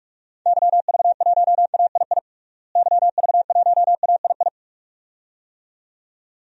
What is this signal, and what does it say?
Morse code, heard as a keyed tone, about 46 wpm, 700 Hz. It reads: YV1AII YV1AII
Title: DXCC-100 Call Signs - 6 Characters - Repeated 2x - 45wpm